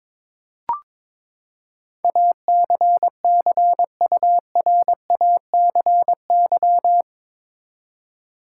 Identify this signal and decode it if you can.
Morse code, heard as a keyed tone, about 21 wpm, 700 Hz. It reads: ACCURACY
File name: practice / 22wpm / Eight Letter Words - 22wpm